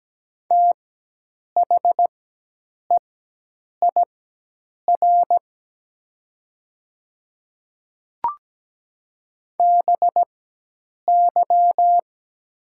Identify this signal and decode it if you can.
Morse code, heard as a keyed tone, about 17 wpm, spaced as for 8 wpm, 700 Hz. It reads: THEIR BY
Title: 4x Character Spacing - Top 100 Words - 17wpm